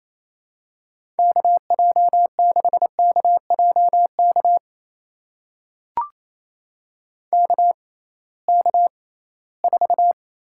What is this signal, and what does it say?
Morse code, heard as a keyed tone, about 28 wpm, spaced as for 18 wpm, 700 Hz. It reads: KJ6KJK KK4
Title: USA Call Signs - 6 Characters - 6x Character Spacing - 28wpm